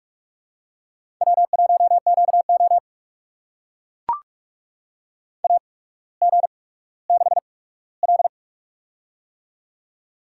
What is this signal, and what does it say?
Morse code, heard as a keyed tone, about 44 wpm, spaced as for 19 wpm, 700 Hz. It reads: W1QO AG6L
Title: USA Call Signs - 4 Characters - 8x Character Spacing - 45wpm